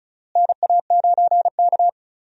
Morse code, heard as a keyed tone, about 34 wpm, 700 Hz. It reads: NA9K
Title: USA Call Signs - 4 Characters - Repeated 2x - 35wpm